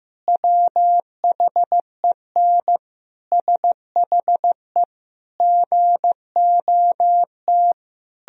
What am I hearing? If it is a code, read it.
Morse code, heard as a keyed tone, about 16 wpm, 700 Hz. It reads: WHEN SHE GOT
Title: Sentences from Top 1000 Words - Review Part 1 - 15wpm